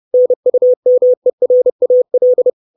Morse code, heard as a keyed tone, about 30 wpm, 500 Hz. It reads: NUMERAL
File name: 632 Words 30WPM